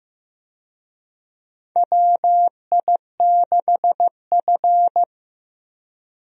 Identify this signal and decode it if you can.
Morse code, heard as a keyed tone, about 15 wpm, 700 Hz. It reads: WI6F
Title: USA Call Signs - 4 Characters - Repeated 2x - 15wpm